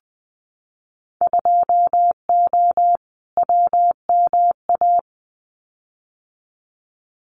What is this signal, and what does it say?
Morse code, heard as a keyed tone, about 20 wpm, 700 Hz. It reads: 2O WMA
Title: Winter Field Day Exchanges Encore 20wpm